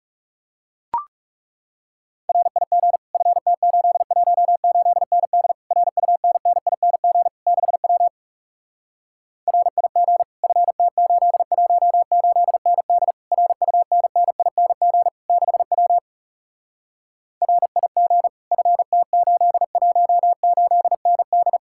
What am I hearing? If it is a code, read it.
Morse code, heard as a keyed tone, about 40 wpm, 700 Hz. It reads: RIG FT818ND RUNNING 6W RIG FT818ND RUNNING 6W RIG FT818ND